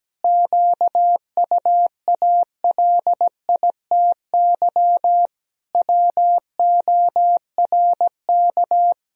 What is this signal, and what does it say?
Morse code, heard as a keyed tone, about 17 wpm, 700 Hz. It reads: QUALITY WORK
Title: Sentences from Top 1000 Words - Review Part 4 - 17wpm